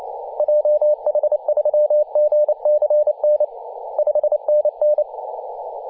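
Morse code, heard as a keyed tone, about 28 wpm, 600 Hz, with radio noise. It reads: JH3GCN 5NN
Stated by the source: DX station in Papua New Guinea on 21.040 MHz in CW.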